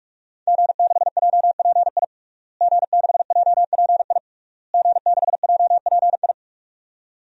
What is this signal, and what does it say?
Morse code, heard as a keyed tone, about 47 wpm, 700 Hz. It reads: G6JPI G6JPI G6JPI
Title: DXCC-100 Call Signs - 5 Characters - Repeated 3x - 45wpm